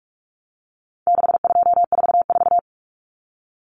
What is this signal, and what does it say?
Morse code, heard as a keyed tone, about 44 wpm, 700 Hz. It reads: 6244
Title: Numbers - 4-Digit 45wpm